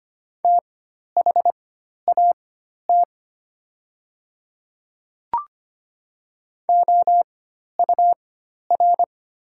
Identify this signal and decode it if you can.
Morse code, heard as a keyed tone, about 25 wpm, spaced as for 12 wpm, 700 Hz. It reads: THAT OUR
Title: 4x Character Spacing - Top 200 Words - 25wpm